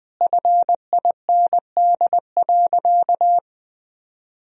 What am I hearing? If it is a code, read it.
Morse code, heard as a keyed tone, about 20 wpm, 700 Hz. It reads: FIND.
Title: Sentences from Top 1000 Words - Review Part 2 - 20wpm